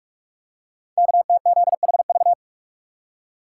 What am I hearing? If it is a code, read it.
Morse code, heard as a keyed tone, about 44 wpm, 700 Hz. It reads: KTZHV
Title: Code Groups - 5 Letters - 45wpm